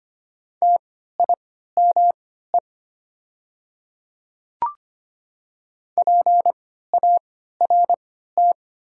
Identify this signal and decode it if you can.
Morse code, heard as a keyed tone, about 25 wpm, spaced as for 14 wpm, 700 Hz. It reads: TIME PART
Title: 3x Character Spacing - Top 500 Words - 25wpm